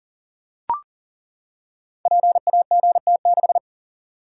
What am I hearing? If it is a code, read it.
Morse code, heard as a keyed tone, about 38 wpm, 700 Hz. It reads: PAGT6
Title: Code Groups - 5 Letter:Numbers - 40wpm